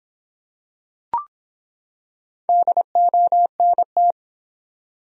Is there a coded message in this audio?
Yes: Morse code from 2.5 to 4.1 s, heard as a keyed tone, about 23 wpm, 700 Hz. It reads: DONT